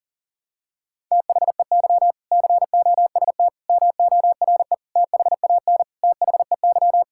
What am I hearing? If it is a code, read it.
Morse code, heard as a keyed tone, about 40 wpm, 700 Hz. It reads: THEY COST MORE THAN THEY
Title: Sentences from Top 1000 Words - Review Part 5 - 40wpm